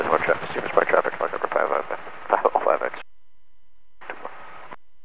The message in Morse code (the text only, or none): none